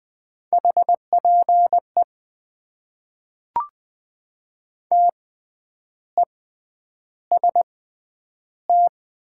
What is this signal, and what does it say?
Morse code, heard as a keyed tone, about 20 wpm, spaced as for 9 wpm, 700 Hz. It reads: HPE TEST
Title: Top 90 QSO Elements - 6x Character Spacing - 20wpm